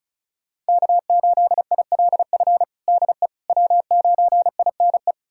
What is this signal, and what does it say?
Morse code, heard as a keyed tone, about 36 wpm, 700 Hz. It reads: K8ILF DE W9INE
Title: USA Callsign DE Callsign - 5 Characters - 35wpm